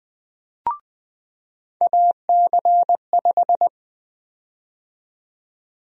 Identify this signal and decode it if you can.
Morse code, heard as a keyed tone, about 21 wpm, 700 Hz. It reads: AC5
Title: USA Call Sign Prefixes - 20wpm